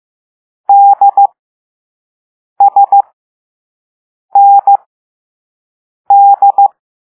Morse code, heard as a keyed tone, about 15 wpm, spaced as for 5 wpm, 800 Hz. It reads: DSND